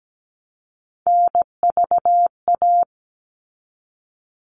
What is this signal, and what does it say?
Morse code, heard as a keyed tone, about 17 wpm, 700 Hz. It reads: NVA
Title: All Cut Numbers - 3-Digit 17wpm